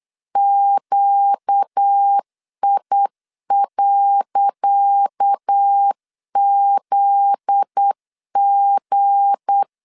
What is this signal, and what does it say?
Morse code, heard as a keyed tone, about 8 wpm, 800 Hz. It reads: QI.ZG
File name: LEZIONI / 10 SIGNS / 10C DOT 8wpm 3649